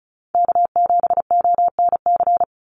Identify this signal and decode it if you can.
Morse code, heard as a keyed tone, about 34 wpm, 700 Hz. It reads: K7ONC
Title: USA Call Signs 35wpm